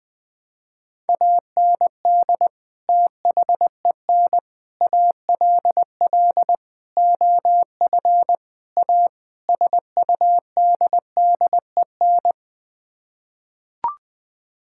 Morse code, heard as a keyed tone, about 20 wpm, 700 Hz. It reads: AND THEN ALL OF A SUDDEN
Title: Sets of 6 Words - 20wpm